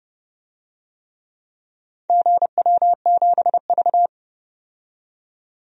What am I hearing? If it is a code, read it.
Morse code, heard as a keyed tone, about 29 wpm, 700 Hz. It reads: GW7V
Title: DXCC-100 Call Signs - 4 Characters - 30wpm